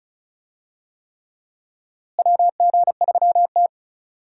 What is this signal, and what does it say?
Morse code, heard as a keyed tone, about 34 wpm, 700 Hz. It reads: WG3T
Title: USA Call Signs - 4 Characters - Repeated 3x - 35wpm